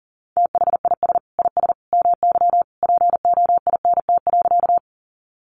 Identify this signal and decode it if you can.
Morse code, heard as a keyed tone, about 40 wpm, 700 Hz. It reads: THIS IS MY POINT.
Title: Sentences from Top 200 Words 40wpm